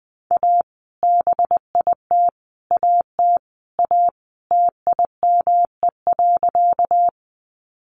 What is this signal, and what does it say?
Morse code, heard as a keyed tone, about 20 wpm, 700 Hz. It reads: A BIT AT A TIME.
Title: Sentences from Top 900 Words 20wpm